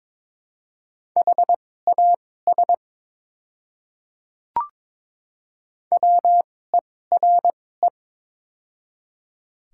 Morse code, heard as a keyed tone, about 22 wpm, spaced as for 16 wpm, 700 Hz. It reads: HAS WERE
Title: 2x Character Spacing - Top 100 Words - 22wpm